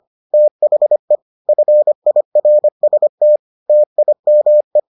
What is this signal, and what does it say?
Morse code, heard as a keyed tone, about 25 wpm, 600 Hz. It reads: THE FIRST TIME